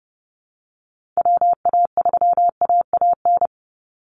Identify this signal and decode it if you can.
Morse code, heard as a keyed tone, about 30 wpm, 700 Hz. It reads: WA3AAN
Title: Call Signs Repeated 2x - 30wpm